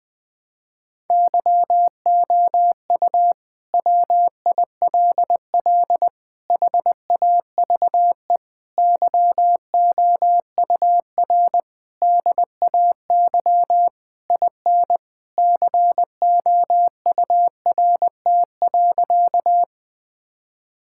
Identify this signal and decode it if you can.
Morse code, heard as a keyed tone, about 21 wpm, 700 Hz. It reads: YOU WILL HAVE YOUR DAY IN COURT.
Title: Sentences from Top 1600 Words 20wpm